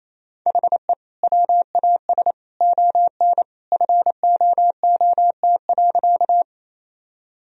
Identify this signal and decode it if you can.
Morse code, heard as a keyed tone, about 27 wpm, 700 Hz. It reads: HE WAS ON FOOT.
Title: Sentences from Top 1000 Words - Review Part 3 - 28wpm